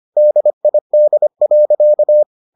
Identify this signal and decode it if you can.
Morse code, heard as a keyed tone, about 25 wpm, 600 Hz. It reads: DID.